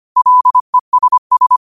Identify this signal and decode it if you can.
Morse code, heard as a keyed tone, about 25 wpm, 1000 Hz. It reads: LESS
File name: qotdfiles / 25wpm